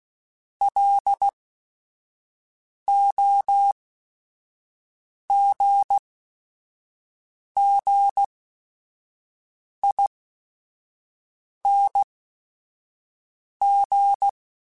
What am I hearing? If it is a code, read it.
Morse code, heard as a keyed tone, about 16 wpm, spaced as for 5 wpm, 800 Hz. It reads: LOGGING
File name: Special INGs / Special INGs-mp3s / Sngl Ing @5